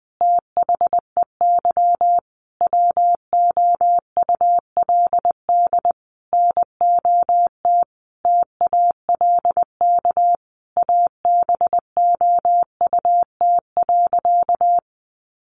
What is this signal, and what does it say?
Morse code, heard as a keyed tone, about 20 wpm, 700 Hz. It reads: THEY WOULD NOT TALK ABOUT.
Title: Sentences from Top 800 Words 20wpm